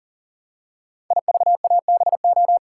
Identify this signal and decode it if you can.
Morse code, heard as a keyed tone, about 40 wpm, 700 Hz. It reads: IVABO
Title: Code Groups - 5 Letters - 40wpm